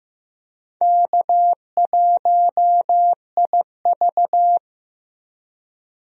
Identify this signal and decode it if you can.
Morse code, heard as a keyed tone, about 16 wpm, 700 Hz. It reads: K1IV
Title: USA Call Signs - 4 Characters - Repeated 3x - 15wpm